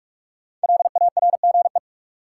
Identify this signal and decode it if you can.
Morse code, heard as a keyed tone, about 46 wpm, 700 Hz. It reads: LARGE
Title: Sets of 4 Words - New Word Drill - 45wpm